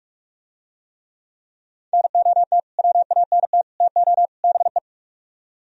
Morse code, heard as a keyed tone, about 44 wpm, 700 Hz. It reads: NOT WANT TO BE